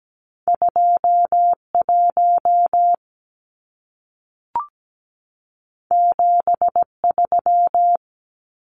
Morse code, heard as a keyed tone, about 17 wpm, 700 Hz. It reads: 21 73
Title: Numbers - 2-Digit 17wpm